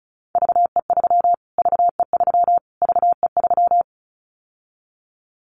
Morse code, heard as a keyed tone, about 34 wpm, 700 Hz. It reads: VE3 VE3 VE3
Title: Callsign Prefixes for Top 25 DXCC - Repeated 3x - 35wpm